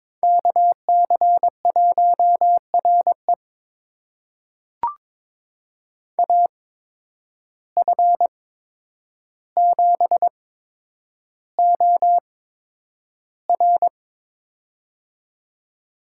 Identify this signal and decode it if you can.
Morse code, heard as a keyed tone, about 22 wpm, spaced as for 9 wpm, 700 Hz. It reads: KC1RE AF7OR
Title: USA Call Signs - 5 Characters - 8x Character Spacing - 22wpm